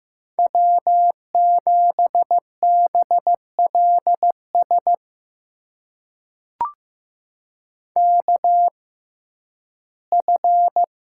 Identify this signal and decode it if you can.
Morse code, heard as a keyed tone, about 15 wpm, spaced as for 11 wpm, 700 Hz. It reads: W7BLS KF
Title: USA Call Signs - 6x Character Spacing - 15wpm